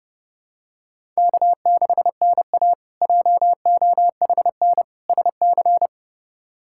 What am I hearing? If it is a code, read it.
Morse code, heard as a keyed tone, about 32 wpm, 700 Hz. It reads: K6NA JOHN SC
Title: K1USN SST QSO Practice - 30wpm